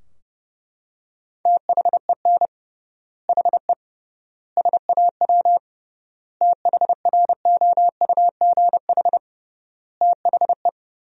Morse code, heard as a keyed tone, about 30 wpm, 700 Hz. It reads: THEN HE SAW THROUGH THE
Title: 3x Word-Spacing - Sentences from Top 400 Words - 30wpm